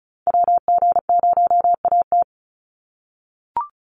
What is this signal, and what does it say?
Morse code, heard as a keyed tone, about 34 wpm, 700 Hz. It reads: WG0AT
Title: NA SOTA Chaser Callsigns 35wpm